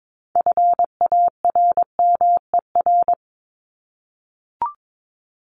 Morse code, heard as a keyed tone, about 22 wpm, 700 Hz. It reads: FARMER
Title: Top 1751-2000 Words 22wpm